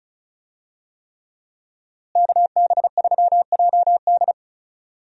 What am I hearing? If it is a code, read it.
Morse code, heard as a keyed tone, about 36 wpm, 700 Hz. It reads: KB3JD
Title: USA Call Signs - 5 Characters - Repeated 3x - 35wpm